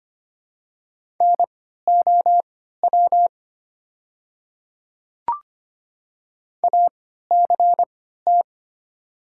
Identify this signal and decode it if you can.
Morse code, heard as a keyed tone, about 25 wpm, spaced as for 14 wpm, 690 Hz. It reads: NOW ACT
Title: 3x Character Spacing - Top 200 Words - 25wpm